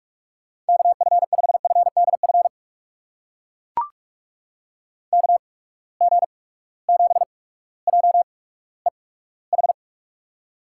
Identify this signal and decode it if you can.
Morse code, heard as a keyed tone, about 43 wpm, spaced as for 19 wpm, 700 Hz. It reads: KF5FDF KG7JEH
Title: USA Call Signs - 8x Character Spacing - 45wpm